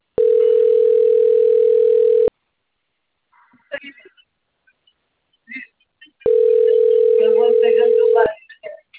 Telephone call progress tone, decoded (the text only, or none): ringback tone